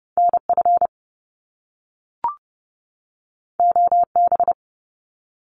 Morse code, heard as a keyed tone, about 30 wpm, 700 Hz. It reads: NF OB
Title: Top Two Letter Combinations in Words 30wpm